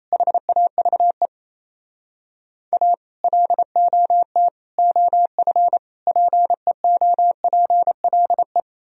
Morse code, heard as a keyed tone, about 38 wpm, 700 Hz. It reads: HAVE A LOT OF PEOPLE